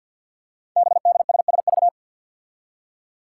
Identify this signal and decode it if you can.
Morse code, heard as a keyed tone, about 53 wpm, 700 Hz. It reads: BDSSV
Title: Five Challenging Letter-Number Combinations 50wpm